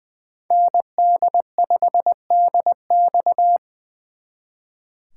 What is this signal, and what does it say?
Morse code, heard as a keyed tone, about 24 wpm, 700 Hz. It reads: ND5DX
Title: Speed Racing - USA 2x2 Call Signs 20wpm